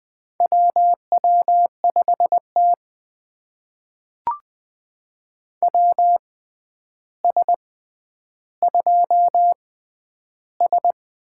Morse code, heard as a keyed tone, about 20 wpm, spaced as for 10 wpm, 700 Hz. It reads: WW5T WS2S
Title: USA Call Signs - 4 Characters - 6x Character Spacing - 20wpm